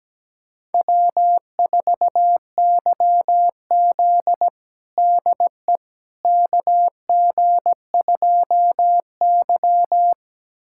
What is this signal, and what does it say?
Morse code, heard as a keyed tone, about 17 wpm, 700 Hz. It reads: W4YZ DE KG2Y
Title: USA Callsign DE Callsign - 4 Characters - 17wpm